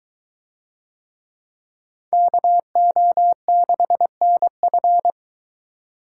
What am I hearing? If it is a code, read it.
Morse code, heard as a keyed tone, about 22 wpm, 700 Hz. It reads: KO6NF